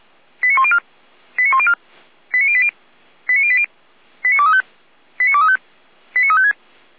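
Five-tone selective call (ZVEI), heard as quick runs of five tones, tones of about 70 ms. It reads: 89104 89104 89080 89080 89235 89235 89356
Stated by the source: ZVEI five-tone call received by SDR, fire brigade Orlamünde (feuerwehr-orlamuende).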